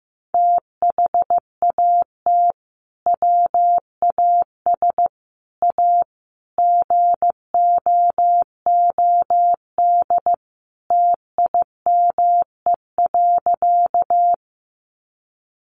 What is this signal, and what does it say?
Morse code, heard as a keyed tone, about 15 wpm, 700 Hz. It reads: THAT WAS A GOOD TIME.
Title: Sentences from Top 200 Words 15wpm